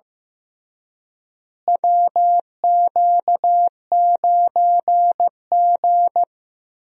Morse code, heard as a keyed tone, about 15 wpm, 700 Hz. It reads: WQ9G
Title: USA Call Signs - 4 Characters - Repeated 3x - 15wpm